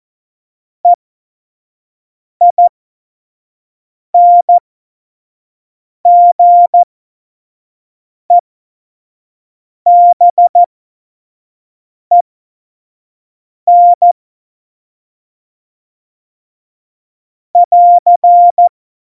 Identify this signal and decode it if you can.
Morse code, heard as a keyed tone, about 14 wpm, spaced as for 5 wpm, 700 Hz. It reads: EINGEBEN <AR>
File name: Morsekurs / Witze / Witz 18 Juli 17, Tempo 25 F70